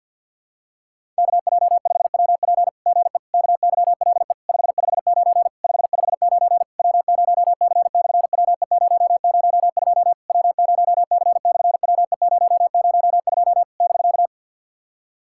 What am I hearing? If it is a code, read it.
Morse code, heard as a keyed tone, about 50 wpm, 700 Hz. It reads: KJ5WP GE KYLE 559 559 W0C/WE002 W0C/WE002 <BK>
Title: SOTA QSO Practice 50wpm